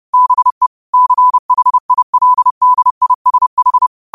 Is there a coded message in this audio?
Yes: Morse code from 0.1 to 3.9 s, heard as a keyed tone, about 30 wpm, 1000 Hz. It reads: BE CHILDISH